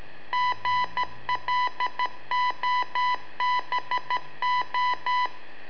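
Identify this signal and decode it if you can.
Morse code, heard as a keyed tone, about 13 wpm, 990 Hz. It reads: GLOBO
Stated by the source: Telemetry in Morse from LU7AA and LU4EZ, AMSAT and Turma Mar del Plata, sent from a balloon.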